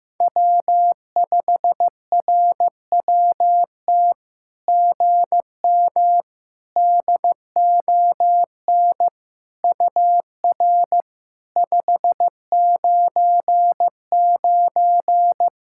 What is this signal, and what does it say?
Morse code, heard as a keyed tone, about 15 wpm, 700 Hz. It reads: W5RWT GM DON UR 599
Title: SOTA QSO Practice 15wpm